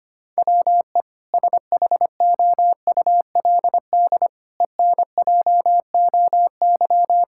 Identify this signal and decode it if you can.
Morse code, heard as a keyed tone, about 25 wpm, 700 Hz. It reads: WE SHOULD ENJOY
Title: Sentences from Top 1000 Words - Review Part 5 - 25wpm